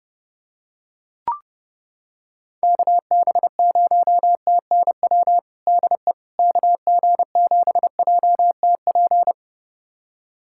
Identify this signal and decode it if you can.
Morse code, heard as a keyed tone, about 30 wpm, 700 Hz. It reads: KB0TNW DE KG7JTP
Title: USA Callsign DE Callsign - 6 Characters - 30wpm